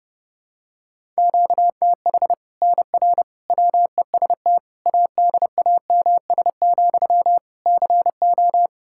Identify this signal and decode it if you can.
Morse code, heard as a keyed tone, about 32 wpm, 700 Hz. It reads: QTH NR WEST ADAMS, CO
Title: Practice QSO - Exchange 1 - Beginning - 30wpm